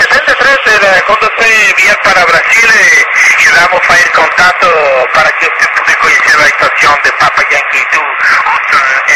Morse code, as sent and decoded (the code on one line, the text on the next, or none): none